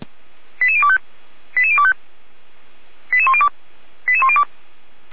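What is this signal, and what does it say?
Five-tone selective call (ZVEI), heard as quick runs of five tones, tones of about 70 ms. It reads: 88025 88025 88102 88102
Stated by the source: ZVEI five-tone selective call received on SDR.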